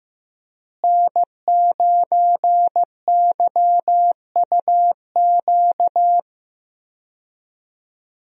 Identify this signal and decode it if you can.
Morse code, heard as a keyed tone, about 16 wpm, 700 Hz. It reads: N9YUQ
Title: USA Call Signs - 5 Characters - Repeated 2x - 15wpm